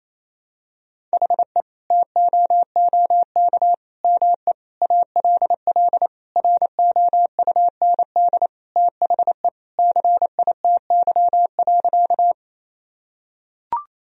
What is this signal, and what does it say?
Morse code, heard as a keyed tone, about 28 wpm, 700 Hz. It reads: HE TOOK ME ALL ROUND THE CITY.
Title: Sentences from Top 1000 Words - Review Part 1 - 28wpm